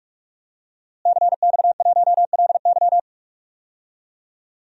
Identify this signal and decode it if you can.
Morse code, heard as a keyed tone, about 46 wpm, 700 Hz. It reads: CX1LY
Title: DXCC-100 Call Signs - 5 Characters - Repeated 2x - 45wpm